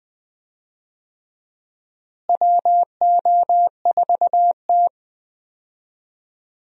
Morse code, heard as a keyed tone, about 20 wpm, 700 Hz. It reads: WO4T
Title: USA Call Signs - 4 Characters - Repeated 3x - 20wpm